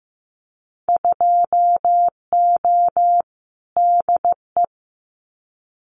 Morse code, heard as a keyed tone, about 15 wpm, 700 Hz. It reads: 2O DE
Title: Winter Field Day Exchanges Encore 15wpm